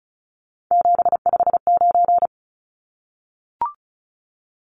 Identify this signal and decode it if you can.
Morse code, heard as a keyed tone, about 34 wpm, 700 Hz. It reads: 759
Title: Numbers - 3-Digit 35wpm